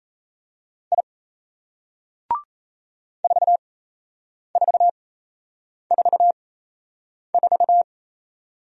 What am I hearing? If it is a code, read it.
Morse code, heard as a keyed tone, about 38 wpm, spaced as for 8 wpm, 700 Hz. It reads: I 4444